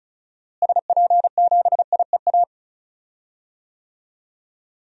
Morse code, heard as a keyed tone, about 34 wpm, 700 Hz. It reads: SP7IEA
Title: DXCC-100 Call Signs - 6 Characters - Repeated 3x - 35wpm